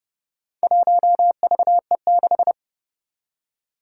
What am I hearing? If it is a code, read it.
Morse code, heard as a keyed tone, about 30 wpm, 700 Hz. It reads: 1VE6
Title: Four Challenging Letter-Number Combinations 30wpm